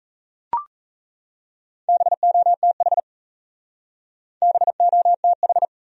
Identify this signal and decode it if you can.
Morse code, heard as a keyed tone, about 40 wpm, 700 Hz. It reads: BOTH BOTH